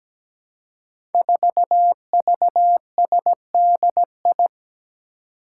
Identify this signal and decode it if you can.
Morse code, heard as a keyed tone, about 17 wpm, 700 Hz. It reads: 4VSDI
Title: Five Challenging Letter-Number Combinations 17wpm